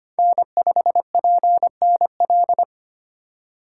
Morse code, heard as a keyed tone, about 25 wpm, 700 Hz. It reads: N5PNL
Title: USA Call Signs - 5 Characters - 3x Character Spacing - 25wpm